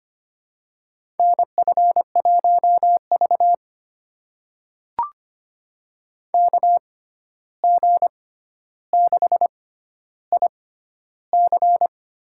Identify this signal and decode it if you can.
Morse code, heard as a keyed tone, about 25 wpm, spaced as for 12 wpm, 700 Hz. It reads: NF1V KG6IC
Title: USA Call Signs - 6x Character Spacing - 25wpm